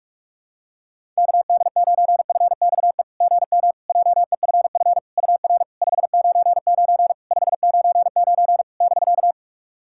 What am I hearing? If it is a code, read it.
Morse code, heard as a keyed tone, about 41 wpm, 690 Hz. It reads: KD9FXE GM JEFF UR 599 599 <BK>
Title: SOTA QSO Practice 45wpm